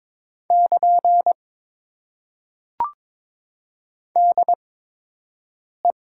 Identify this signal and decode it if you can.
Morse code, heard as a keyed tone, about 22 wpm, spaced as for 6 wpm, 700 Hz. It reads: <KN> DE
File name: Top 90 QSO Elements - 8x Character Spacing - 22wpm